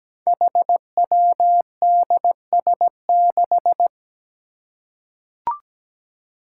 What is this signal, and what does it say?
Morse code, heard as a keyed tone, about 17 wpm, 700 Hz. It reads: HWDS6
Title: Five Challenging Letter-Number Combinations 17wpm